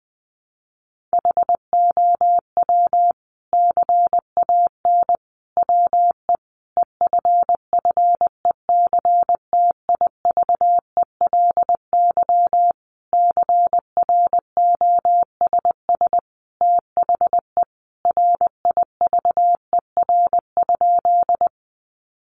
Morse code, heard as a keyed tone, about 20 wpm, 700 Hz. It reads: HOW CAN WE EFFECTIVELY CROSS THE RIVER?
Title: Sentences from Top 1400 Words 20wpm